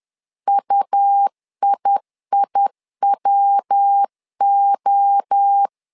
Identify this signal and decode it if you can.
Morse code, heard as a keyed tone, about 10 wpm, 800 Hz. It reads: UIIWO
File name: LEZIONI / 4 Q 10wpm 2210